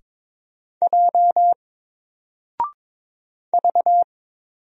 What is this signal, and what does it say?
Morse code, heard as a keyed tone, about 22 wpm, 700 Hz. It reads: J V
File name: Single-Letter-Number-22wpm